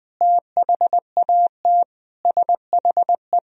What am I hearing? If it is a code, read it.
Morse code, heard as a keyed tone, about 19 wpm, 700 Hz. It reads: THAT SHE
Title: Sentences from Top 1000 Words - Review Part 3 - 20wpm